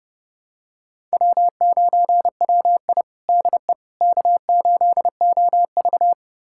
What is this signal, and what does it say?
Morse code, heard as a keyed tone, about 30 wpm, 700 Hz. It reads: W9WI DE K8OV
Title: USA Callsign DE Callsign - 4 Characters - 30wpm